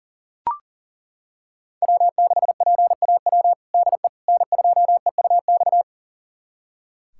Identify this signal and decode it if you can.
Morse code, heard as a keyed tone, about 37 wpm, 700 Hz. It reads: W6PAW DE N2EUX
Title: USA Callsign DE Callsign - 5 Characters - 40wpm